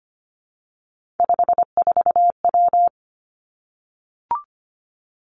Morse code, heard as a keyed tone, about 25 wpm, 700 Hz. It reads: 54W
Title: Three Challenging Letter-Number Combinations 25wpm